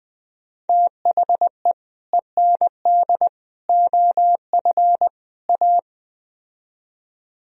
Morse code, heard as a keyed tone, about 20 wpm, 700 Hz. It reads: THE END OF A
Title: Speed-Racing - Sets of 4 Words 17wpm